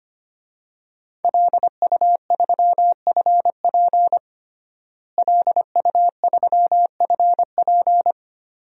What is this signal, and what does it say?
Morse code, heard as a keyed tone, about 25 wpm, 700 Hz. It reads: LU3FP LU3FP
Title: DXCC-100 Call Signs - 5 Characters - Repeated 2x - 25wpm